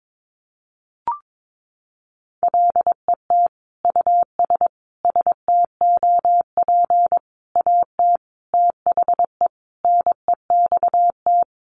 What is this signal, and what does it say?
Morse code, heard as a keyed tone, about 22 wpm, 700 Hz. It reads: LET US STOP AT THE NEXT
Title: Sentences from Top 400 Words 22wpm